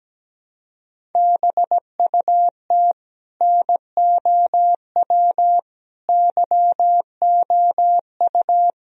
Morse code, heard as a keyed tone, about 17 wpm, 700 Hz. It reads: BUT NOW YOU
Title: Sentences from Top 1000 Words - Review Part 1 - 17wpm